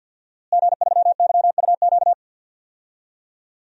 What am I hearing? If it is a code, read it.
Morse code, heard as a keyed tone, about 50 wpm, 700 Hz. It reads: G3YUQ
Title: DXCC-100 Call Signs - 5 Characters - Repeated 3x - 50wpm